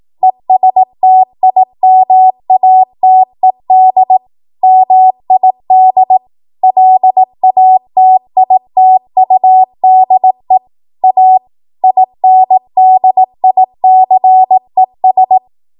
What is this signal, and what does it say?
Morse code, heard as a keyed tone, about 18 wpm, 750 Hz. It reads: ESTIMATED MID LATITUDE A INDICES
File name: Archive / 18 WPM / 18WPM